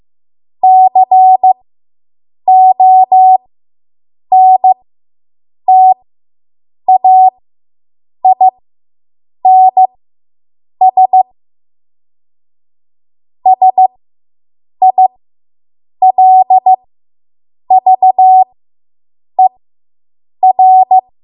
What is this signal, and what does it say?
Morse code, heard as a keyed tone, about 15 wpm, spaced as for 7 wpm, 750 Hz. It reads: CONTAINS SILVER